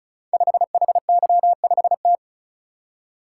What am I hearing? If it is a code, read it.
Morse code, heard as a keyed tone, about 36 wpm, 700 Hz. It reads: 5HY5T
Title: Code Groups - 5 Letter:Numbers - 35wpm